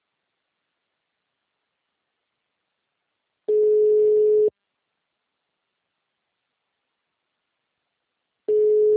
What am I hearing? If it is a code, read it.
Ringback tone (Europe, 425 Hz).